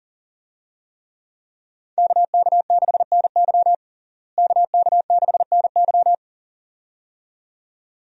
Morse code, heard as a keyed tone, about 40 wpm, 700 Hz. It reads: KK6NY KK6NY
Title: USA Call Signs - 5 Characters - Repeated 2x - 40wpm